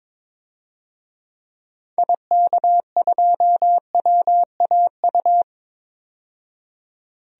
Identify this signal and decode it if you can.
Morse code, heard as a keyed tone, about 22 wpm, 700 Hz. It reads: IK2WAU
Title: DXCC-100 Call Signs - 6 Characters - 22wpm